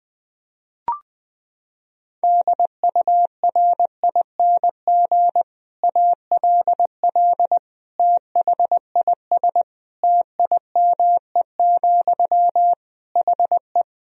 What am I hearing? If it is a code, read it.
Morse code, heard as a keyed tone, about 20 wpm, 700 Hz. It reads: DURING ALL THIS TIME, HE
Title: Sentences from Top 1000 Words - Review Part 3 - 20wpm